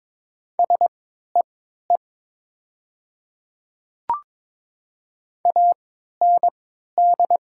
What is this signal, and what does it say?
Morse code, heard as a keyed tone, about 22 wpm, spaced as for 12 wpm, 700 Hz. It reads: SEE AND
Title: 3x Character Spacing - Top 100 Words - 22wpm